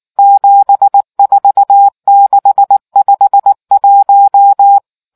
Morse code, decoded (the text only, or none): 74651